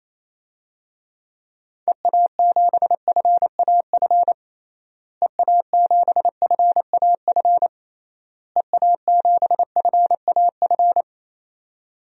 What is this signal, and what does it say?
Morse code, heard as a keyed tone, about 28 wpm, 700 Hz. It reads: EA7FAF EA7FAF EA7FAF
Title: DXCC-100 Call Signs - 6 Characters - Repeated 3x - 28wpm